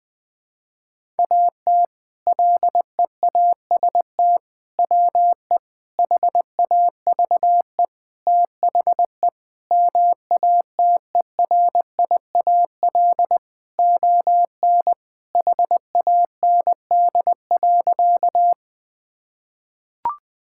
Morse code, heard as a keyed tone, about 20 wpm, 700 Hz. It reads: AT LEAST WE HAVE THE MATERIAL ON HAND.
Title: Sentences from Top 1000 Words - Review Part 5 - 20wpm